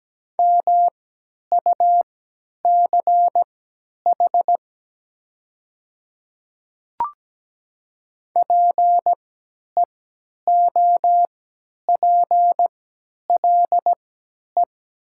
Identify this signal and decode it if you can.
Morse code, heard as a keyed tone, about 17 wpm, spaced as for 10 wpm, 700 Hz. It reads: MUCH PEOPLE
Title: 3x Character Spacing - Top 200 Words - 17wpm